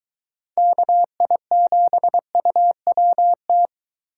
Morse code, heard as a keyed tone, about 22 wpm, 700 Hz. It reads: KI7UWT